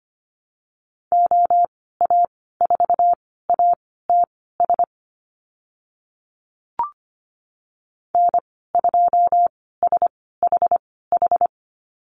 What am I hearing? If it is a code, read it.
Morse code, heard as a keyed tone, about 25 wpm, spaced as for 16 wpm, 700 Hz. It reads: OA4ATS N2SHH
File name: Lesson 15g - Number 5 - Call Signs 25wpm